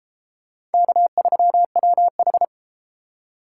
Morse code, heard as a keyed tone, about 31 wpm, 700 Hz. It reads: K3WH